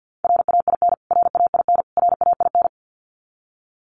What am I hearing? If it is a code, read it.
Morse code, heard as a keyed tone, about 50 wpm, 700 Hz. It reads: RAIN RAIN RAIN
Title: Top 500 Words Repeated 50wpm